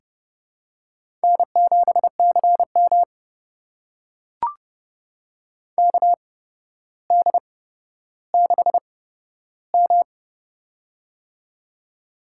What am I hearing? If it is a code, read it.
Morse code, heard as a keyed tone, about 30 wpm, spaced as for 13 wpm, 700 Hz. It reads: N7CM KD6M
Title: USA Call Signs - 4 Characters - 8x Character Spacing - 30wpm